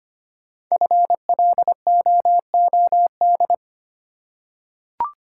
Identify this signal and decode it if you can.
Morse code, heard as a keyed tone, about 25 wpm, 700 Hz. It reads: FLOOD
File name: 5 Letter Words - First 3 Letters Given - 25wpm